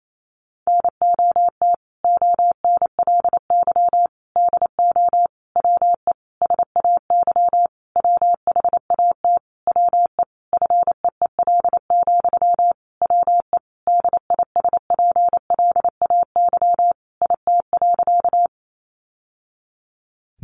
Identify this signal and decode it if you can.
Morse code, heard as a keyed tone, about 28 wpm, 700 Hz. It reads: NOT ONLY DO WE SAY WHAT WE FEEL, WE DISPLAY IT.
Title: Sentences from Top 1200 Words 28wpm